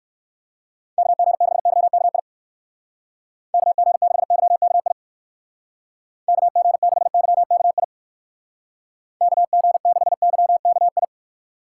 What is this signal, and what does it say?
Morse code, heard as a keyed tone, about 56 wpm, 700 Hz. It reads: KG6YKI KG6YKI KG6YKI KG6YKI
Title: Speed Racing - USA 2x3 Call Signs 45wpm